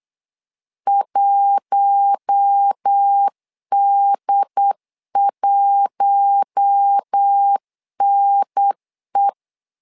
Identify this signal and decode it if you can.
Morse code, heard as a keyed tone, about 8 wpm, 800 Hz. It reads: 1D1NE
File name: LEZIONI / 4 Q1 8wpm 3035